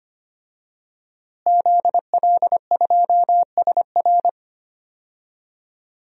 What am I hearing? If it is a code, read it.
Morse code, heard as a keyed tone, about 25 wpm, 700 Hz. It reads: ZL2SR
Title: DXCC-100 Call Signs - 5 Characters - Repeated 3x - 25wpm